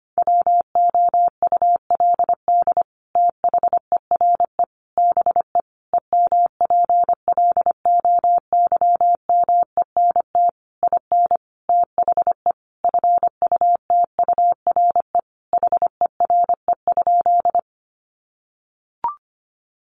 Morse code, heard as a keyed tone, about 25 wpm, 700 Hz. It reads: WOULD THERE BE EMPLOYMENT IN THE FUTURE HERE?
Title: Sentences from Top 1500 Words 25wpm